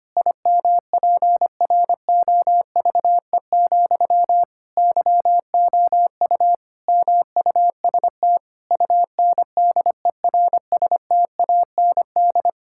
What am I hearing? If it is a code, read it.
Morse code, heard as a keyed tone, about 25 wpm, 700 Hz. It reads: IMPROVE, YOU MUST UNDERSTAND
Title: Sentences from Top 1000 Words - Review Part 5 - 25wpm